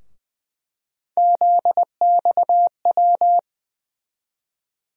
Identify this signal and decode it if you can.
Morse code, heard as a keyed tone, about 20 wpm, 700 Hz. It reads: ZXW
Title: USA Call Sign Postfixes - 3 Letters 20wpm